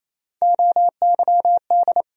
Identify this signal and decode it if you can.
Morse code, heard as a keyed tone, about 28 wpm, 700 Hz. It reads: OYD
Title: USA Call Sign Postfixes - 3 Letters 28wpm